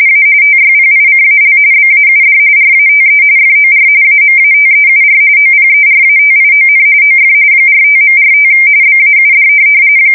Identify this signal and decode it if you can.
RTTY, about 45.45 baud. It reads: A TYPICAL BAUD RATE FOR AMATEUR OPERATION OF 45.45 BAUD
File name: fsk170 / teletype-RTTY